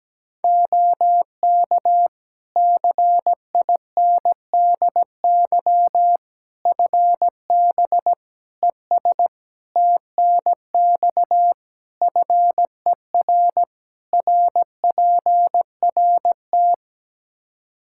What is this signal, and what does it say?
Morse code, heard as a keyed tone, about 17 wpm, 700 Hz. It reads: OK CINDY FB ES TNX FER RPRT
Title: Practice QSO - Exchange 2 - Info - 17wpm